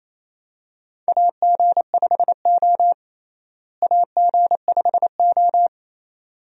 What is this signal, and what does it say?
Morse code, heard as a keyed tone, about 28 wpm, 700 Hz. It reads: AG5O AG5O
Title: USA Call Signs - 4 Characters - Repeated 2x - 28wpm